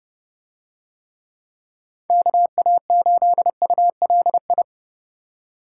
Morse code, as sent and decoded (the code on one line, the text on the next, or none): -.- .- ---.. ..- .-.. ..
KA8ULI